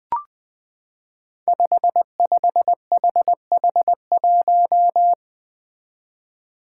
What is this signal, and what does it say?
Morse code, heard as a keyed tone, about 20 wpm, 700 Hz. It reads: 55HH1
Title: Five Challenging Letter-Number Combinations 20wpm